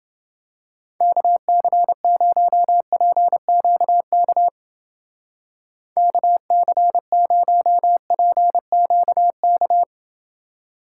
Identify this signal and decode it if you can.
Morse code, heard as a keyed tone, about 29 wpm, 700 Hz. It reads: KC0PQK KC0PQK